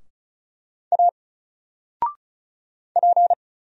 Morse code, heard as a keyed tone, about 36 wpm, 700 Hz. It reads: A P
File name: Single-Letter-Number-35wpm